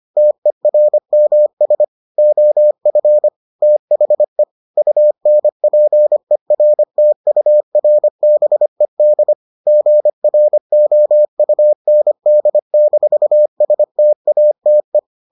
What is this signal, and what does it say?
Morse code, heard as a keyed tone, about 25 wpm, 600 Hz. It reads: TERMS OF THE UNPERTURBED GROUND-STATE